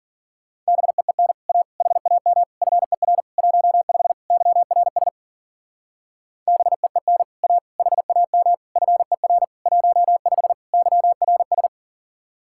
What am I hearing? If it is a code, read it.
Morse code, heard as a keyed tone, about 49 wpm, 700 Hz. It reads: BEEN A HAM FER 15 YRS BEEN A HAM FER 15 YRS
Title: Speed-Racing Common QSO Phrases - 40wpm